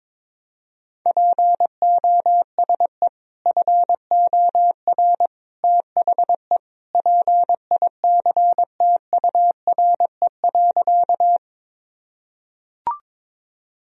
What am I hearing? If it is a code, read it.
Morse code, heard as a keyed tone, about 21 wpm, 700 Hz. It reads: POSE FOR THE PICTURE.
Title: Sentences from Top 1000 Words - Review Part 2 - 22wpm